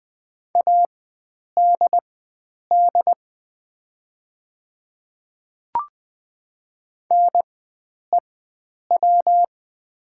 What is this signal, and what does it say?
Morse code, heard as a keyed tone, about 20 wpm, spaced as for 9 wpm, 700 Hz. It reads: ADD NEW
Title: 4x Character Spacing - Top 300 Words - 20wpm